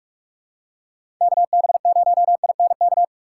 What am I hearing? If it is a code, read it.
Morse code, heard as a keyed tone, about 44 wpm, 700 Hz. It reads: KB0INK
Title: USA Call Signs - 6x Character Spacing - 45wpm